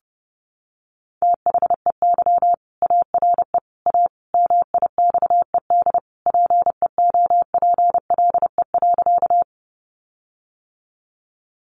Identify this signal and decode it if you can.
Morse code, heard as a keyed tone, about 30 wpm, 700 Hz. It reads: THEY ARE A MIXED PEOPLE.
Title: Sentences from Top 1500 Words 30wpm